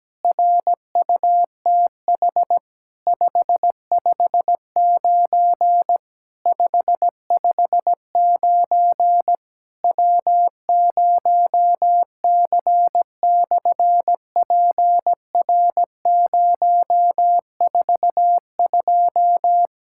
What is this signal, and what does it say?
Morse code, heard as a keyed tone, about 17 wpm, 700 Hz. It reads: RUTH 559 559 W0C/PR042
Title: SOTA QSO Practice 17wpm